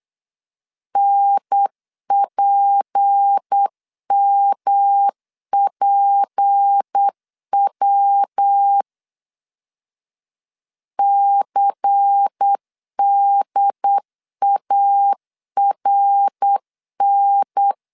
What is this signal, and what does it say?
Morse code, heard as a keyed tone, about 8 wpm, 800 Hz. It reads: NPMPW CDARN
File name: LEZIONI / 6 PKX0 / 6 P 8wpm 3025